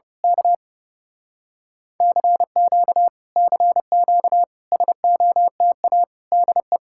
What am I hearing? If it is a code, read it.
Morse code, heard as a keyed tone, about 29 wpm, 700 Hz. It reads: K CQ CQ SOTA DE
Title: Speed-Racing Common QSO Phrases Encore - 30wpm